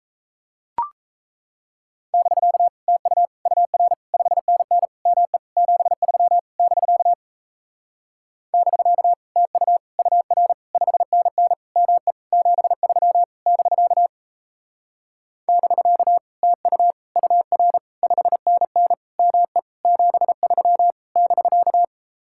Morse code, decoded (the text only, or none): <BK> TU UR 5NN ME 73 <BK> <BK> TU UR 5NN ME 73 <BK> <BK> TU UR 5NN ME 73 <BK>